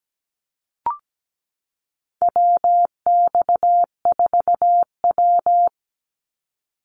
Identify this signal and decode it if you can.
Morse code, heard as a keyed tone, about 17 wpm, 700 Hz. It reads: WX4W
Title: Call Signs Repeated 3x - 17wpm